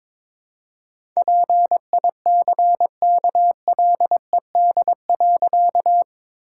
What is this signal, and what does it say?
Morse code, heard as a keyed tone, about 22 wpm, 700 Hz. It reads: PICKLED.
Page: morse-code-ninja-practice-22wpm